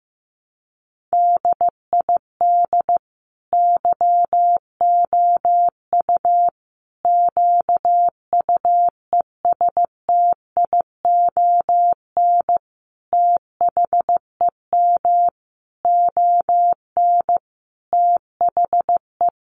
Morse code, heard as a keyed tone, about 15 wpm, 700 Hz. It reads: DID YOU QUESTION THEM ON THE
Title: Sentences from Top 800 Words 15wpm